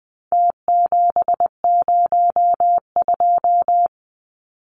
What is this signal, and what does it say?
Morse code, heard as a keyed tone, about 20 wpm, 700 Hz. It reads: T702
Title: Numbers - Contest Style - 4-Digit 20wpm